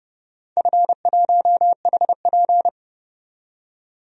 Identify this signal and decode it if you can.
Morse code, heard as a keyed tone, about 30 wpm, 700 Hz. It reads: F1HP
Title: DXCC-100 Call Signs - 4 Characters - 30wpm